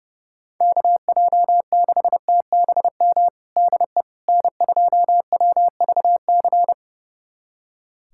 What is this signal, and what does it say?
Morse code, heard as a keyed tone, about 32 wpm, 700 Hz. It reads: KJ6TBM DE N2WVC
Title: USA Callsign DE Callsign 30wpm